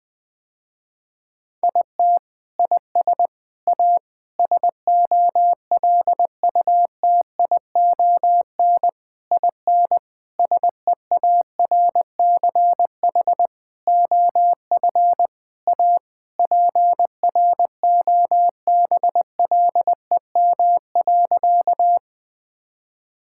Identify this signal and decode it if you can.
Morse code, heard as a keyed tone, about 20 wpm, 700 Hz. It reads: IT IS A SOLUTION IN SEARCH OF A PROBLEM.
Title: Sentences with 7-Word Phrases - Spoken Beginning - 20wpm